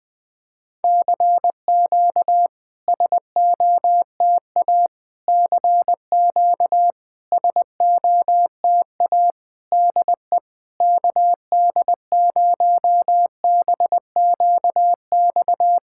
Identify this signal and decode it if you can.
Morse code, heard as a keyed tone, about 22 wpm, 700 Hz. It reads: CQ SOTA CQ SOTA DE KD0BQX
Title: SOTA QSO Practice 20wpm